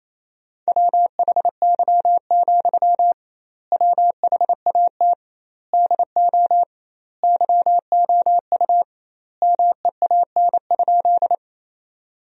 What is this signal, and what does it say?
Morse code, heard as a keyed tone, about 28 wpm, 700 Hz. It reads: WHY, WHAT DO YOU MEAN?
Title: 2x Word-Spacing - Sentences from Top 200 Words - 28wpm